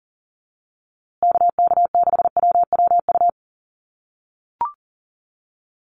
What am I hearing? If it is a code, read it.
Morse code, heard as a keyed tone, about 40 wpm, 700 Hz. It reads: KK6WWU
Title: USA Call Signs Repeated 3x 40wpm